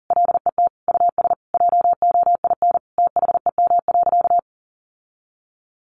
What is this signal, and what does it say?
Morse code, heard as a keyed tone, about 40 wpm, 700 Hz. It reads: LET US JOIN THEM.
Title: Sentences from Top 1400 Words 40wpm